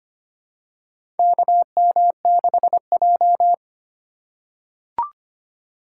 Morse code, heard as a keyed tone, about 25 wpm, 700 Hz. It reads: KM6J
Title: USA Call Signs - 3x Character Spacing - 25wpm